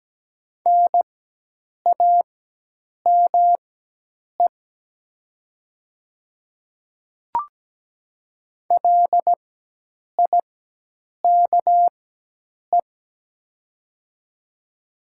Morse code, heard as a keyed tone, about 17 wpm, spaced as for 8 wpm, 700 Hz. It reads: NAME LIKE
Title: 4x Character Spacing - Top 300 Words - 17wpm